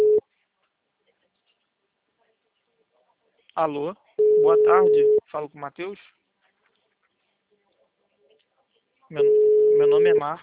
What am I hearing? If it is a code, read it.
Ringback tone (Europe, 425 Hz), with speech.